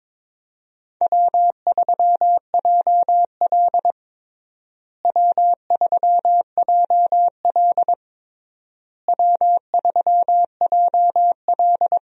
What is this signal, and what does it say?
Morse code, heard as a keyed tone, about 22 wpm, 700 Hz. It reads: W3JL W3JL W3JL
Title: USA Call Signs - 4 Characters - Repeated 3x - 22wpm